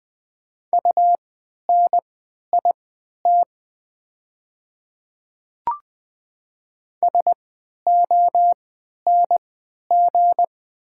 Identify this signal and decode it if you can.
Morse code, heard as a keyed tone, about 20 wpm, spaced as for 11 wpm, 700 Hz. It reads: UNIT SONG
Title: 3x Character Spacing - Top 500 Words - 20wpm